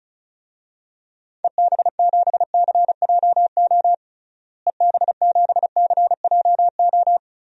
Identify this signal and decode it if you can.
Morse code, heard as a keyed tone, about 34 wpm, 700 Hz. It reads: EB7CJO EB7CJO
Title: DXCC-100 Call Signs - 6 Characters - Repeated 3x - 35wpm